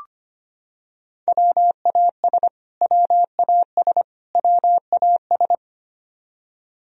Morse code, heard as a keyed tone, about 25 wpm, 700 Hz. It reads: WAS WAS WAS
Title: Top 100 Words Repeated 25wpm